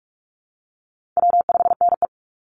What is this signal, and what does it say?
Morse code, heard as a keyed tone, about 44 wpm, 700 Hz. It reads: W5NE
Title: Call Signs Repeated 2x - 45wpm